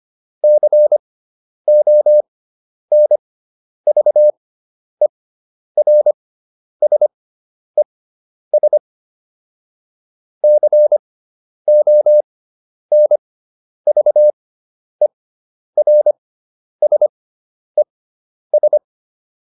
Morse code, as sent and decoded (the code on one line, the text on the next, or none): -.-. --- -. ...- . .-. ... ... / -.-. --- -. ...- . .-. ... ...
CONVERSES CONVERSES